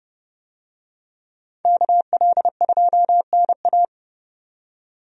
Morse code, heard as a keyed tone, about 30 wpm, 700 Hz. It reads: KL2NA
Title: USA Call Signs - 5 Characters - 30wpm